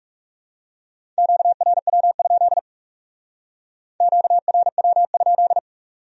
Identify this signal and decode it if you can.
Morse code, heard as a keyed tone, about 39 wpm, 700 Hz. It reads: QRW? QRW?